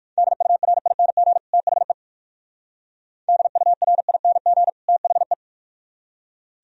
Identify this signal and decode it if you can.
Morse code, heard as a keyed tone, about 55 wpm, 700 Hz. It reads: DURING THE DURING THE
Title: Speed-Racing - Sets of 2 Words 45wpm